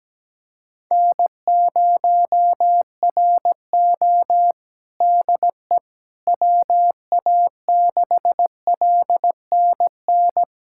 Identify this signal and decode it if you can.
Morse code, heard as a keyed tone, about 17 wpm, 700 Hz. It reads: N0RO DE WA6LNN
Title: USA Callsign DE Callsign 17wpm